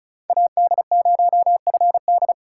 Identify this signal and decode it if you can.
Morse code, heard as a keyed tone, about 34 wpm, 700 Hz. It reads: AD0FD
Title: USA Call Signs - 5 Characters - 3x Character Spacing - 35wpm